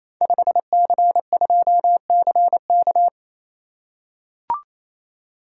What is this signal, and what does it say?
Morse code, heard as a keyed tone, about 28 wpm, 700 Hz. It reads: 5C2CK
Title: Code Groups - 5 Letter:Numbers - 28wpm